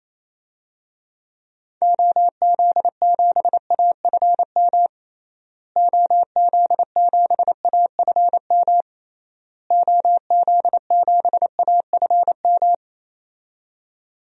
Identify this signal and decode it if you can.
Morse code, heard as a keyed tone, about 28 wpm, 700 Hz. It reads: OZ7AFM OZ7AFM OZ7AFM
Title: DXCC-100 Call Signs - 6 Characters - Repeated 3x - 28wpm